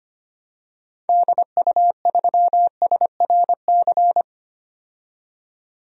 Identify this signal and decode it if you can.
Morse code, heard as a keyed tone, about 25 wpm, 700 Hz. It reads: DU3SRC
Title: DXCC-100 Call Signs - 6 Characters - 25wpm